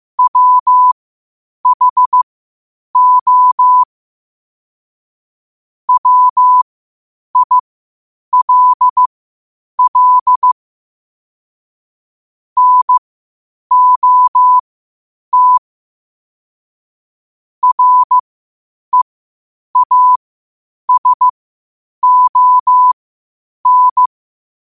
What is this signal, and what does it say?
Morse code, heard as a keyed tone, about 15 wpm, spaced as for 8 wpm, 1000 Hz. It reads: WHO WILL NOT REASON
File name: qotdfiles / 8wpm